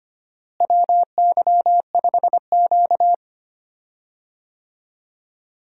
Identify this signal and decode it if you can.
Morse code, heard as a keyed tone, about 25 wpm, 700 Hz. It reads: WY5Q
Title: USA Call Signs - 4 Characters - Repeated 2x - 25wpm